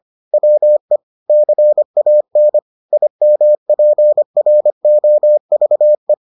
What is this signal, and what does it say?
Morse code, heard as a keyed tone, about 25 wpm, 600 Hz. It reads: WE CAN IMPROVE